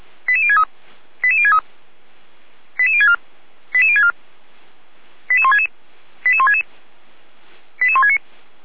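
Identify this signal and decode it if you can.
Five-tone selective call (ZVEI), heard as quick runs of five tones, tones of about 70 ms. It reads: 88063 88063 88064 88064 88166 88166 88169
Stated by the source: ZVEI five-tone selective call received by SDR.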